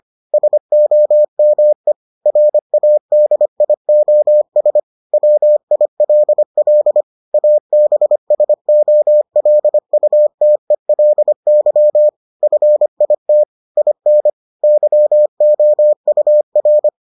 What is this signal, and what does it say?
Morse code, heard as a keyed tone, about 25 wpm, 600 Hz. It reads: SOME RADIOS WILL ABSOLUTELY FIT IN YOUR